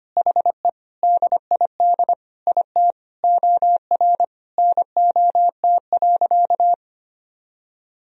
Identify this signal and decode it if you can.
Morse code, heard as a keyed tone, about 25 wpm, 700 Hz. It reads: HE DID IT OR NOT.
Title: Sentences from Top 1000 Words - Review Part 3 - 25wpm